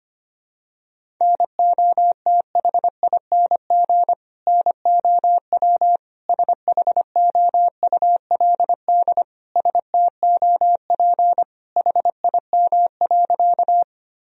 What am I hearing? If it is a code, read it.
Morse code, heard as a keyed tone, about 25 wpm, 700 Hz. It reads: NOTHING NOW SHOULD STOP HIM.
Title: Sentences from Top 1000 Words - Review Part 1 - 25wpm